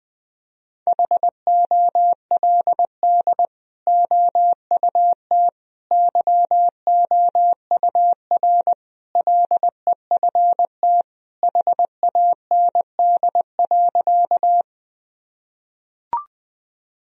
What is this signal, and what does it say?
Morse code, heard as a keyed tone, about 20 wpm, 700 Hz. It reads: HOLD OUT YOUR LEFT HAND.
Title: Sentences from Top 1000 Words - Review Part 2 - 20wpm